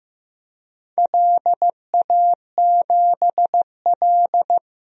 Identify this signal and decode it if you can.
Morse code, heard as a keyed tone, about 15 wpm, 700 Hz. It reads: LA7L
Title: DXCC-100 Call Signs - 4 Characters - Repeated 2x - 15wpm